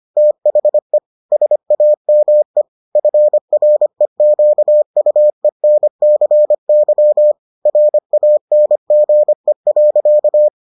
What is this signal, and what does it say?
Morse code, heard as a keyed tone, about 25 wpm, 600 Hz. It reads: THE SAME FREQUENCY RANGE.